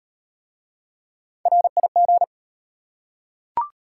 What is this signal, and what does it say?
Morse code, heard as a keyed tone, about 35 wpm, 700 Hz. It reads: RIG